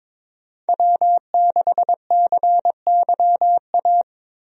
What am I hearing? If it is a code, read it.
Morse code, heard as a keyed tone, about 22 wpm, 700 Hz. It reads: W6CYA
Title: USA Call Signs - 5 Characters - Repeated 2x - 22wpm